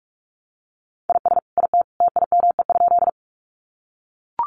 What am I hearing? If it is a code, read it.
Morse code, heard as a keyed tone, about 44 wpm, 700 Hz. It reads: IS IT TIME?
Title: Sentences from Top 100 Words 45wpm